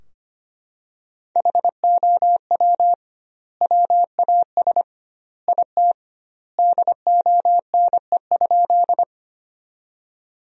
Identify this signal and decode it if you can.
Morse code, heard as a keyed tone, about 25 wpm, 700 Hz. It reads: HOW WAS IT DONE?
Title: 2x Word-Spacing - Sentences from Top 500 Words - 25wpm